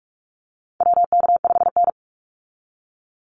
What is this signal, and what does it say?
Morse code, heard as a keyed tone, about 46 wpm, 700 Hz. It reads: WK5N
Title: Call Signs - 45wpm